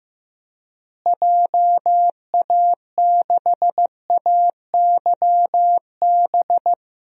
Morse code, heard as a keyed tone, about 15 wpm, 700 Hz. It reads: JA6AYB
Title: DXCC-100 Call Signs - 6 Characters - Repeated 2x - 15wpm